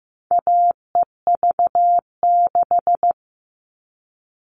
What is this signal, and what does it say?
Morse code, heard as a keyed tone, about 15 wpm, 700 Hz. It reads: AEV6
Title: All Cut Numbers - 4-Digit 15wpm